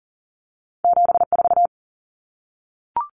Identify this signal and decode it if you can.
Morse code, heard as a keyed tone, about 40 wpm, 700 Hz. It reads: 74
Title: Common Cut Numbers - 2-Digit 40wpm